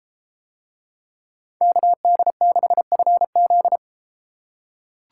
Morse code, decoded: KD6FZ

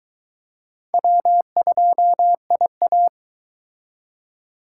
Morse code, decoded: W2IA

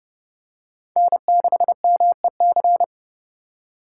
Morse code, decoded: N6MEC

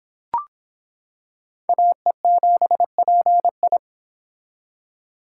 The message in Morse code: AE7PI

